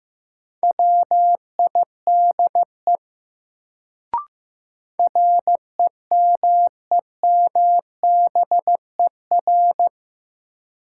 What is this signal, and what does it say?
Morse code, heard as a keyed tone, about 15 wpm, 700 Hz. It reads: WIDE REMEMBER
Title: Sets of 4 Words - New Word Drill - 15wpm